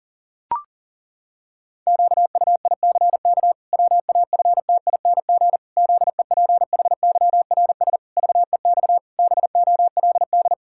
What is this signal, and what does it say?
Morse code, heard as a keyed tone, about 37 wpm, 700 Hz. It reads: QUICK WAFTING ZEPHYRS VEX BOLD
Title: Pangram Challenge v2 - 40wpm